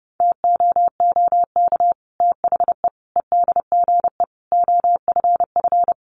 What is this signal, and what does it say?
Morse code, heard as a keyed tone, about 30 wpm, 700 Hz. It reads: TOOK THE EDGE OFF
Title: Sentences from Top 1300 Words 30wpm